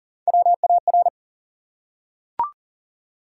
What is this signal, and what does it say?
Morse code, heard as a keyed tone, about 40 wpm, 700 Hz. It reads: WAR
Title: Three Letter Words - 40wpm